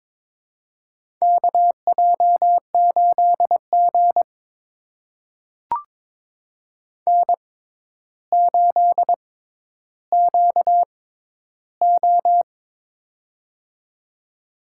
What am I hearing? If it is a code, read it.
Morse code, heard as a keyed tone, about 22 wpm, spaced as for 11 wpm, 700 Hz. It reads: KJ8G N8QO
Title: USA Call Signs - 4 Characters - 6x Character Spacing - 22wpm